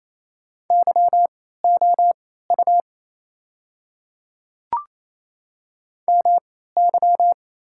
Morse code, heard as a keyed tone, about 29 wpm, spaced as for 16 wpm, 690 Hz. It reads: YOU MY
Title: 3x Character Spacing - Top 300 Words - 28wpm